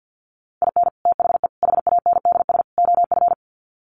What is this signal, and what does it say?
Morse code, heard as a keyed tone, about 50 wpm, 700 Hz. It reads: IN THE HANDS OF
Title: Sets of 4 Words 50wpm